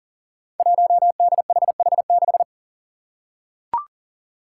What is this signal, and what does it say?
Morse code, heard as a keyed tone, about 40 wpm, 700 Hz. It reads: 1DHH6
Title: Five Challenging Letter-Number Combinations 40wpm